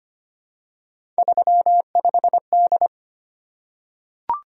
Morse code, heard as a keyed tone, about 25 wpm, 700 Hz. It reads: 35D
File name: Three Letter-Number Combinations - 25wpm